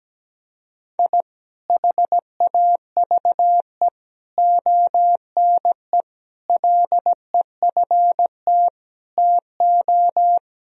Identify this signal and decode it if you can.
Morse code, heard as a keyed tone, about 17 wpm, 700 Hz. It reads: I HAVE ONE LEFT TO
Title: Sentences from Top 1000 Words - Review Part 4 - 17wpm